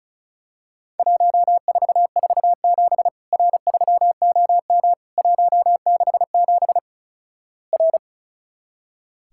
Morse code, heard as a keyed tone, about 36 wpm, 700 Hz. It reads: 1447 R3OM 167 R
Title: Worked All Europe DX Contest - QTC Exchange - 35wpm